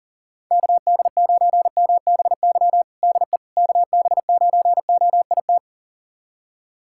Morse code, heard as a keyed tone, about 40 wpm, 700 Hz. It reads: KD9MBY DE KB9OIT
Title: USA Callsign DE Callsign 40wpm